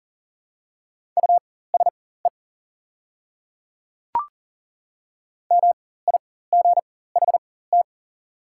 Morse code, heard as a keyed tone, about 40 wpm, spaced as for 19 wpm, 690 Hz. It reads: USE MIGHT
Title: 4x Character Spacing - Top 300 Words - 40wpm